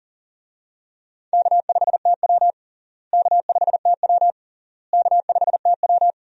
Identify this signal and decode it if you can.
Morse code, heard as a keyed tone, about 37 wpm, 700 Hz. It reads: K5TW K5TW K5TW
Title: USA Call Signs - 4 Characters - Repeated 3x - 40wpm